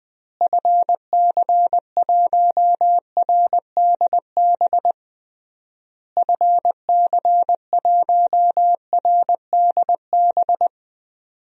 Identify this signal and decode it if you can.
Morse code, heard as a keyed tone, about 19 wpm, 700 Hz. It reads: FC1RDB FC1RDB
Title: DXCC-100 Call Signs - 6 Characters - Repeated 3x - 20wpm